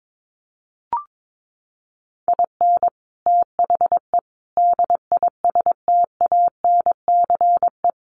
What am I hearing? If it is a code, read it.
Morse code, heard as a keyed tone, about 22 wpm, 700 Hz. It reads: IN THE DISTANCE
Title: Sentences from Top 1200 Words 22wpm